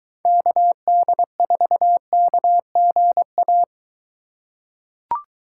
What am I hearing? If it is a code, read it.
Morse code, heard as a keyed tone, about 22 wpm, 700 Hz. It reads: KD4KGA